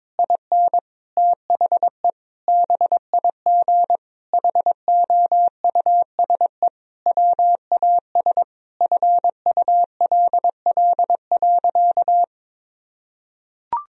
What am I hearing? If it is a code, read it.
Morse code, heard as a keyed tone, about 22 wpm, 700 Hz. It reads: IN THE BIG HOUSE WAS FULL.
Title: Sentences from Top 1000 Words - Review Part 3 - 22wpm